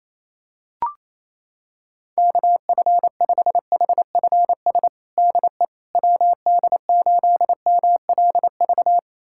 Morse code, heard as a keyed tone, about 28 wpm, 700 Hz. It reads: KF5HFS DE WD8MLV
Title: USA Callsign DE Callsign - 6 Characters - 28wpm